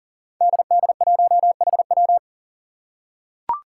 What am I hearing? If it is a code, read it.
Morse code, heard as a keyed tone, about 37 wpm, 700 Hz. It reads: DD1HW
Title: Five Challenging Letter-Number Combinations 40wpm